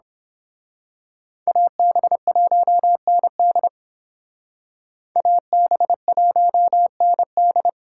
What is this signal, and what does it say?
Morse code, heard as a keyed tone, about 26 wpm, 700 Hz. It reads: AB1ND AB1ND
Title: Speed Racing - USA 2x2 Call Signs 22wpm